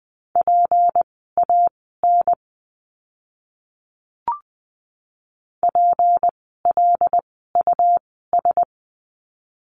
Morse code, heard as a keyed tone, about 20 wpm, spaced as for 14 wpm, 700 Hz. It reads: PAN PLUS
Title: 153 Lesson 25f - Letter p - Words 20wpm